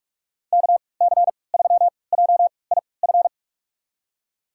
Morse code, heard as a keyed tone, about 47 wpm, spaced as for 26 wpm, 700 Hz. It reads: KC3JIF
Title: USA Call Signs - 6 Characters - 3x Character Spacing - 45wpm